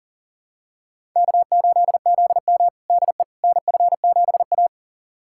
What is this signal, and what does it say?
Morse code, heard as a keyed tone, about 40 wpm, 700 Hz. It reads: K8ZM DE NF7A
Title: USA Callsign DE Callsign - 4 Characters - 40wpm